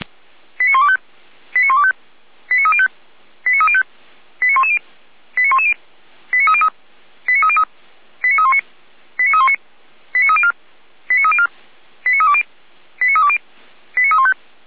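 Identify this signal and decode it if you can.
Five-tone selective call (ZVEI), heard as quick runs of five tones, tones of about 80 ms. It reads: 89125 89125 89305 89305 89110 89110 89303 89303 89319 89319 89304 89304 89320 89320 89315
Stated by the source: Five-tone ZVEI selective call received by SDR.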